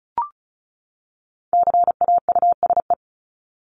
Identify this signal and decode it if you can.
Morse code, heard as a keyed tone, about 34 wpm, 700 Hz. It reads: CAUSE
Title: ED Suffix - Top 100 Words with Base Words 35wpm